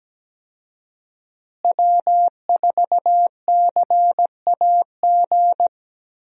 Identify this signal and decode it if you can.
Morse code, heard as a keyed tone, about 17 wpm, 700 Hz. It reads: W4CAG
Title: USA Call Signs - 5 Characters - Repeated 2x - 17wpm